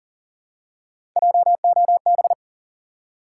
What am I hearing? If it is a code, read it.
Morse code, heard as a keyed tone, about 41 wpm, 700 Hz. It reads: JOB
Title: Three Letter Words - 40wpm